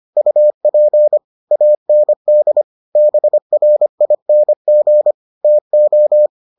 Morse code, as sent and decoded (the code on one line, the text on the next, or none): ..- .--. / .- -. -.. / -... .-. .. -. --. / - ---
UP AND BRING TO